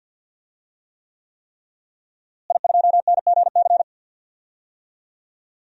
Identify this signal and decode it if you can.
Morse code, heard as a keyed tone, about 46 wpm, 700 Hz. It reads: I2NGC
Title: DXCC-100 Call Signs - 5 Characters - Repeated 3x - 50wpm